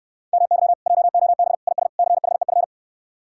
Morse code, heard as a keyed tone, about 69 wpm, 700 Hz. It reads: MY JOB IS CHEF